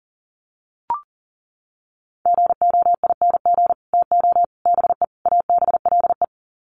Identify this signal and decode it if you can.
Morse code, heard as a keyed tone, about 40 wpm, 700 Hz. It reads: GOING TO BE ABLE
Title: Sets of 4 Words 40wpm